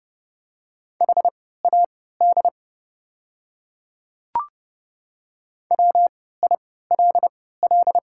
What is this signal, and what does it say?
Morse code, heard as a keyed tone, about 32 wpm, spaced as for 17 wpm, 700 Hz. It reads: HAD WILL
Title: 3x Character Spacing - Top 100 Words - 30wpm